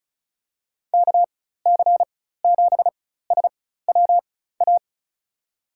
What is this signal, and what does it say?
Morse code, heard as a keyed tone, about 36 wpm, spaced as for 16 wpm, 690 Hz. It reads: KC7SWA
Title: USA Call Signs - 6 Characters - 4x Character Spacing - 35wpm